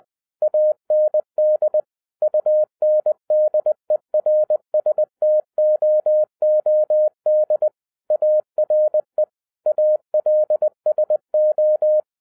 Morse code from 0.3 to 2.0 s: AND